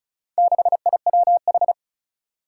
Morse code, heard as a keyed tone, about 36 wpm, 700 Hz. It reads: 6IWH
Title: Four Challenging Letter-Number Combinations 35wpm